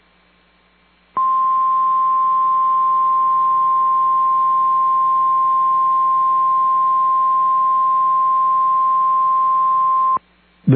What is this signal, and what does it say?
Emergency Alert System (SAME) tones (the 1050 Hz attention tone). It is heard from 1.1 to 10.2 s.